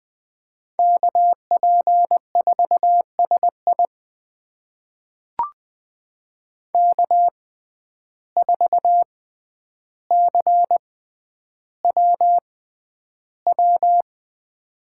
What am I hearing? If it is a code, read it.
Morse code, heard as a keyed tone, about 20 wpm, spaced as for 10 wpm, 700 Hz. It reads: KP4SI K4CWW
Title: USA Call Signs - 5 Characters - 6x Character Spacing - 20wpm